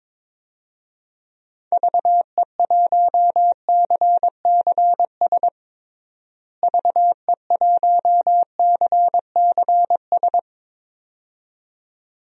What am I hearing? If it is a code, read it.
Morse code, heard as a keyed tone, about 22 wpm, 700 Hz. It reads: VE1CCS VE1CCS
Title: DXCC-100 Call Signs - 6 Characters - Repeated 2x - 22wpm